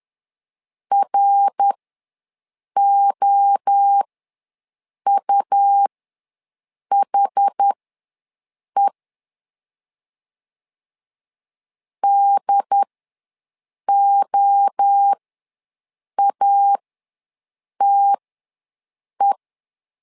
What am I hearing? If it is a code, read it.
Morse code, heard as a keyed tone, about 10 wpm, spaced as for 6 wpm, 800 Hz. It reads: ROUHE DOATE
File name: LEZIONI / 3 MWRO / 3 MWRO ws 10wpm 2310